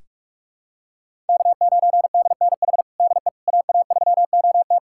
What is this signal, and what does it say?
Morse code, heard as a keyed tone, about 47 wpm, 700 Hz. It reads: K9DNH DE AA3OT
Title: USA Callsign DE Callsign - 5 Characters - 45wpm